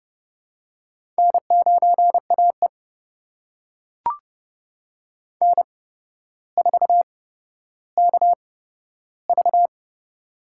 Morse code, heard as a keyed tone, about 31 wpm, spaced as for 13 wpm, 700 Hz. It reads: N9AE N4KV
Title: USA Call Signs - 4 Characters - 8x Character Spacing - 30wpm